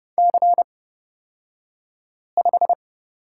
Morse code, heard as a keyed tone, about 31 wpm, spaced as for 5 wpm, 700 Hz. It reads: C5
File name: Lesson 38d - Letter q - Two Characters 30wpm